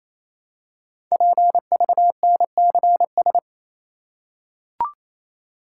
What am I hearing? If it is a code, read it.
Morse code, heard as a keyed tone, about 27 wpm, 700 Hz. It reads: PVNCS
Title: Code Groups - 5 Letter:Numbers - 28wpm